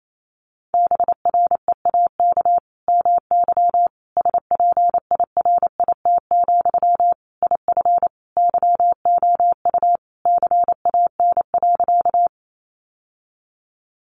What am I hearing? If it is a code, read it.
Morse code, heard as a keyed tone, about 28 wpm, 700 Hz. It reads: BREAK MY SPIRIT, IF YOU CAN.
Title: Sentences from Top 1400 Words 28wpm